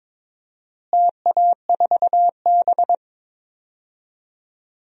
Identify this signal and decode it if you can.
Morse code, heard as a keyed tone, about 22 wpm, 700 Hz. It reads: TA4B
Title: DXCC-100 Call Signs - 4 Characters - Repeated 2x - 22wpm